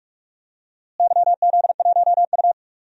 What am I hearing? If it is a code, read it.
Morse code, heard as a keyed tone, about 41 wpm, 700 Hz. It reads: YZ1U
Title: DXCC-100 Call Signs - 4 Characters - Repeated 2x - 45wpm